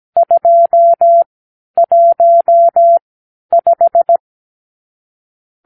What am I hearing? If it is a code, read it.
Morse code, heard as a keyed tone, about 17 wpm, spaced as for 10 wpm, 680 Hz. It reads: 215
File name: k7qo / Lesson 52 (8)